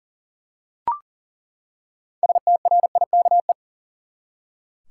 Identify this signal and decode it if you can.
Morse code, heard as a keyed tone, about 40 wpm, 700 Hz. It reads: STRIKE
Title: Six Letter Words - 40wpm